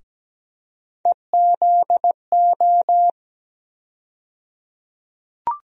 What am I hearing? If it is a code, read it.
Morse code, heard as a keyed tone, about 17 wpm, 700 Hz. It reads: EZO
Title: USA Call Sign Postfixes - 3 Letters 17wpm